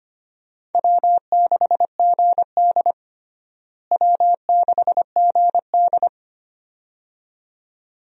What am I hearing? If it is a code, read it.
Morse code, heard as a keyed tone, about 25 wpm, 700 Hz. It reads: W6GD W6GD
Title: USA Call Signs - 4 Characters - Repeated 2x - 25wpm